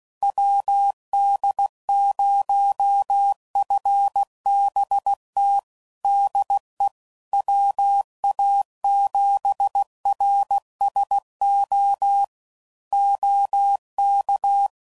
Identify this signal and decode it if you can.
Morse code, heard as a keyed tone, about 16 wpm, 800 Hz. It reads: WD0FBT DE WA7RSO OK
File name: CW Code Groups @15WPM / QSOs / WD0FBT Spaced